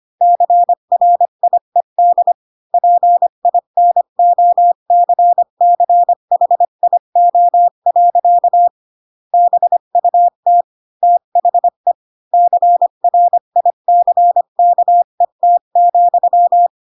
Morse code, heard as a keyed tone, about 25 wpm, 700 Hz. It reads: CRIED PINOCCHIO. BUT THE CRICKET,
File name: Pinocchio / Pinocchio ch4 20 25 ik1hge